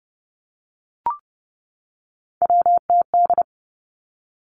Morse code, heard as a keyed tone, about 30 wpm, 700 Hz. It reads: WTD
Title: Three Challenging Letter-Number Combinations 30wpm